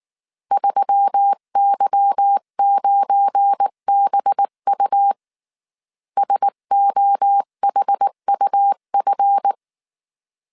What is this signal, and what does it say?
Morse code, heard as a keyed tone, about 19 wpm, 800 Hz. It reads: 3Y9BU SOHUF
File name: CW / DET / 1 9 1500C S C 18wpm 2539